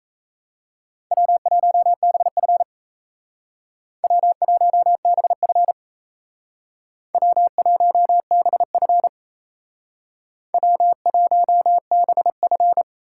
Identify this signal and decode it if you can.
Morse code, heard as a keyed tone, about 37 wpm, 700 Hz. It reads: W1BF W1BF W1BF W1BF